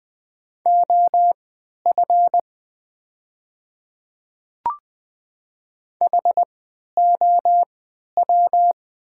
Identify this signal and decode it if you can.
Morse code, heard as a keyed tone, about 20 wpm, spaced as for 11 wpm, 700 Hz. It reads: OF HOW
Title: 3x Character Spacing - Top 100 Words - 20wpm